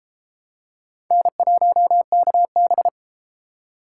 Morse code, heard as a keyed tone, about 31 wpm, 700 Hz. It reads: N1KB